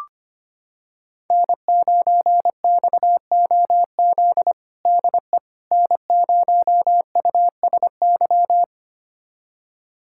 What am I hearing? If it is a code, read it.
Morse code, heard as a keyed tone, about 25 wpm, 700 Hz. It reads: N9XOZ DE N0USY
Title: USA Callsign DE Callsign - 5 Characters - 25wpm